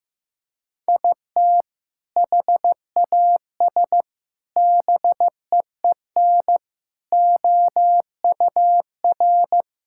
Morse code, heard as a keyed tone, about 15 wpm, 700 Hz. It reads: IT HAS BEEN OUR
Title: Sentences from Top 1000 Words - Review Part 5 - 15wpm